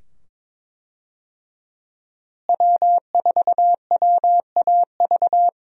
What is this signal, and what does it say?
Morse code, heard as a keyed tone, about 22 wpm, 700 Hz. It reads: W4WAV
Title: USA Call Signs - 5 Characters - Repeated 2x - 22wpm